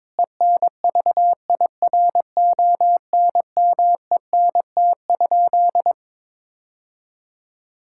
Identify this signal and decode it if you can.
Morse code, heard as a keyed tone, about 22 wpm, 700 Hz. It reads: ENVIRONMENT?
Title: Sentences from Top 1000 Words - Review Part 5 - 22wpm